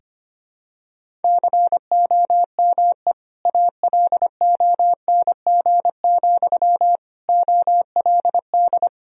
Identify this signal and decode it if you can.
Morse code, heard as a keyed tone, about 25 wpm, 700 Hz. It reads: COME ALONG, OLD
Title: Sentences from Top 1000 Words - Review Part 2 - 25wpm